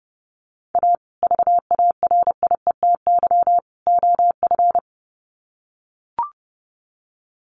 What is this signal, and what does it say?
Morse code, heard as a keyed tone, about 30 wpm, 700 Hz. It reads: A VARIETY OF
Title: Sets of 3 Words 30wpm